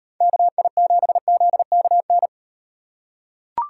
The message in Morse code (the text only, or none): KI7ZKN